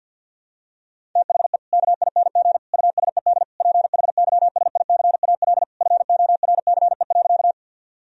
Morse code, heard as a keyed tone, about 50 wpm, 700 Hz. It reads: THE KING USED PHYSICAL FORCE.
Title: Sentences from Top 1000 Words - Review Part 4 - 50wpm